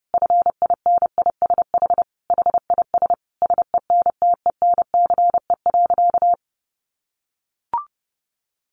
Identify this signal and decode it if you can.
Morse code, heard as a keyed tone, about 28 wpm, 700 Hz. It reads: FINISH HIS SENTENCE.
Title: Sentences from Top 1200 Words 30wpm